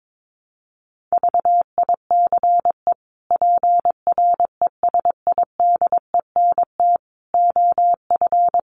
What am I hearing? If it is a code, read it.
Morse code, heard as a keyed tone, about 22 wpm, 700 Hz. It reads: VICE PRESIDENT OF
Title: Sets of 5 Words Encore2 22wpm